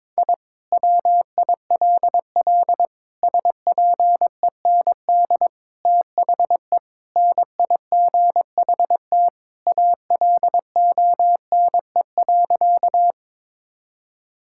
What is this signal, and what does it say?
Morse code, heard as a keyed tone, about 22 wpm, 700 Hz. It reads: I WILL SPEND THE NIGHT ALONE.
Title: Sentences from Top 1000 Words - Review Part 5 - 22wpm